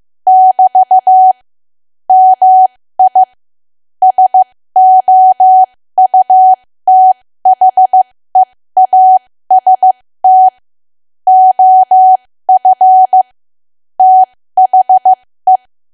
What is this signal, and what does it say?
Morse code, heard as a keyed tone, about 15 wpm, spaced as for 13 wpm, 750 Hz. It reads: <BT> MI SOUTHEAST OF THE
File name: cwa / W1AW-Level2 / 13WPM